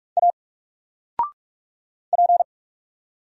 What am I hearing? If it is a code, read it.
Morse code, heard as a keyed tone, about 44 wpm, 700 Hz. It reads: A P
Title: Single Letter-Number 45wpm